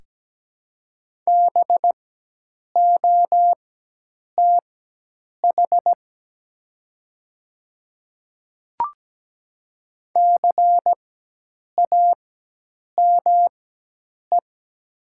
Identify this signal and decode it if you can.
Morse code, heard as a keyed tone, about 17 wpm, spaced as for 8 wpm, 700 Hz. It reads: BOTH CAME
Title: 4x Character Spacing - Top 300 Words - 17wpm